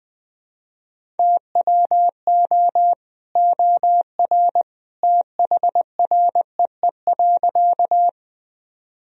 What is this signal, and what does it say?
Morse code, heard as a keyed tone, about 20 wpm, 700 Hz. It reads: TWO OR THREE.
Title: Sentences from Top 1000 Words - Review Part 1 - 20wpm